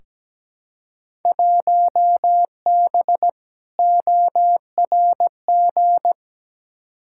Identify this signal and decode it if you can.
Morse code, heard as a keyed tone, about 17 wpm, 700 Hz. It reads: 1B ORG
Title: Field Day Exchanges Encore - 17wpm